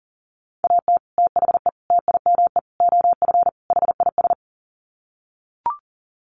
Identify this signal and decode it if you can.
Morse code, heard as a keyed tone, about 40 wpm, 700 Hz. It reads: AT THE TIME OF HIS
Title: Sets of 5 Words Encore2 40wpm